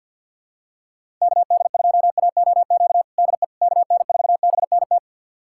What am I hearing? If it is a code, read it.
Morse code, heard as a keyed tone, about 51 wpm, 700 Hz. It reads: KD2AOQ DE KN4BNT
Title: USA Callsign DE Callsign - 6 Characters - 50wpm